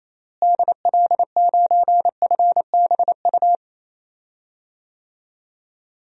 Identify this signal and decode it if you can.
Morse code, heard as a keyed tone, about 28 wpm, 700 Hz. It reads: DL9FBU
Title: DXCC-100 Call Signs - 6 Characters - Repeated 3x - 28wpm